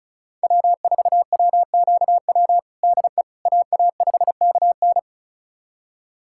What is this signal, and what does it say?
Morse code, heard as a keyed tone, about 36 wpm, 700 Hz. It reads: W4WQW DE AA5KN
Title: USA Callsign DE Callsign - 5 Characters - 35wpm